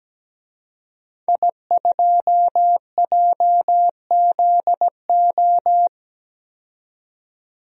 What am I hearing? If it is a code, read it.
Morse code, heard as a keyed tone, about 17 wpm, 700 Hz. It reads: I2JZO
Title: DXCC-100 Call Signs - 5 Characters - 17wpm